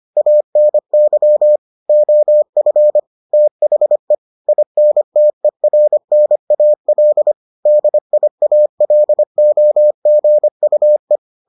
Morse code, heard as a keyed tone, about 25 wpm, 600 Hz. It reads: ANY OF THE INTERNAL DIALOGUE